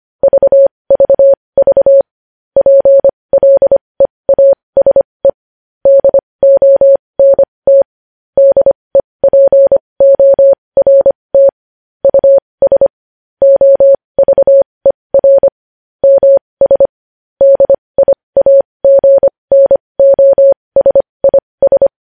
Morse code, heard as a keyed tone, about 25 wpm, spaced as for 20 wpm, 550 Hz. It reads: VVV PLEASE DONT DEPORT US OVER MS DIAGNOSIS
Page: News Headlines in Morse Code at 20 WPM